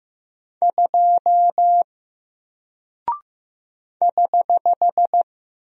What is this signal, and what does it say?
Morse code, heard as a keyed tone, about 15 wpm, 700 Hz. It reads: 2 <HH>
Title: Single Character (Letter:Number:Punctuation:Prosign) 15wpm